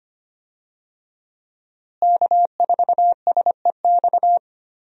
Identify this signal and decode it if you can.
Morse code, heard as a keyed tone, about 25 wpm, 700 Hz. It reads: K4SEX
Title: USA Call Signs - 5 Characters - Repeated 3x - 25wpm